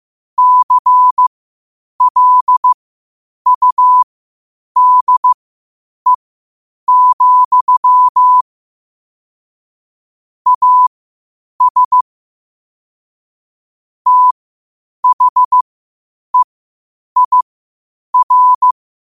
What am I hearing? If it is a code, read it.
Morse code, heard as a keyed tone, about 15 wpm, spaced as for 8 wpm, 1000 Hz. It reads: CLUDE, AS THEIR
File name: qotdfiles / 8wpm